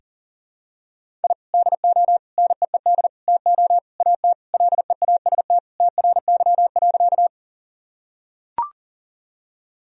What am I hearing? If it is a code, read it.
Morse code, heard as a keyed tone, about 37 wpm, 700 Hz. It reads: I DO NEED TO AT LEAST TRY.
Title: Sentences from Top 1000 Words - Review Part 1 - 40wpm